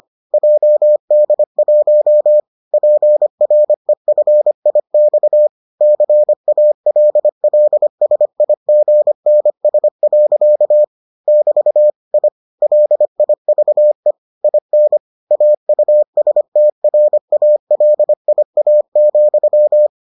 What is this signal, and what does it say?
Morse code, heard as a keyed tone, about 25 wpm, 600 Hz. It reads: JD1 PREFIX CALLSIGNS. <BT> I LIVE IN AUSTRALIA,